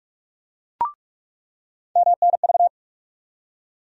Morse code, heard as a keyed tone, about 44 wpm, 700 Hz. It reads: MNV
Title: USA Call Sign Postfixes - 3 Letters 45wpm